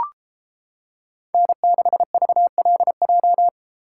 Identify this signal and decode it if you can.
Morse code, heard as a keyed tone, about 28 wpm, 700 Hz. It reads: N6VLJ